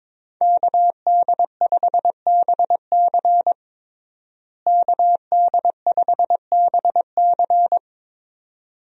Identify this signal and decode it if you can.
Morse code, heard as a keyed tone, about 22 wpm, 700 Hz. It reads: KD5BC KD5BC
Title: USA Call Signs - 5 Characters - Repeated 2x - 22wpm